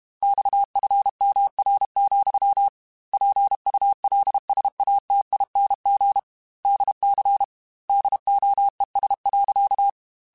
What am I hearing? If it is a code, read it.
Morse code, heard as a keyed tone, about 32 wpm, 800 Hz. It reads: XFMR, PULSATING DC DOES.
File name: Technical Subjects / Technical Subjects-mp3s / Sw Pwr Sup @30